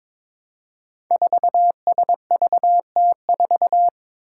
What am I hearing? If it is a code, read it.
Morse code, heard as a keyed tone, about 22 wpm, 700 Hz. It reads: 4SVT4
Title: Five Challenging Letter-Number Combinations 22wpm